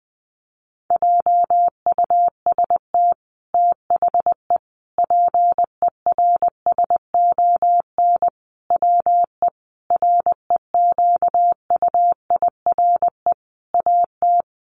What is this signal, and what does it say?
Morse code, heard as a keyed tone, about 20 wpm, 700 Hz. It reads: JUST THE PERSON WE REQUIRE AT
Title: Sentences from Top 900 Words 20wpm